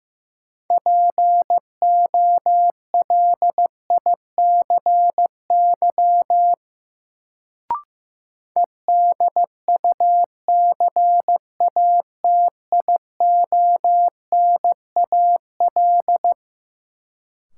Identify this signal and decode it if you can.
Morse code, heard as a keyed tone, about 15 wpm, 700 Hz. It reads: POLICY EDUCATIONAL
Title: Sentences from Top 1200 Words - New Word Drill Alt - 15wpm